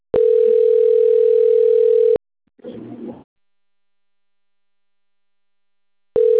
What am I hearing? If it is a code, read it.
Ringback tone (North America), 440 + 480 Hz.